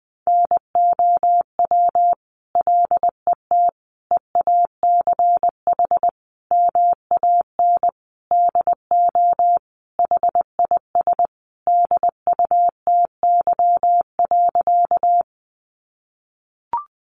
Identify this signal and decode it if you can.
Morse code, heard as a keyed tone, about 20 wpm, 700 Hz. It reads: NOW LET EACH MAN DO HIS DUTY.
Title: Sentences from Top 1500 Words 20wpm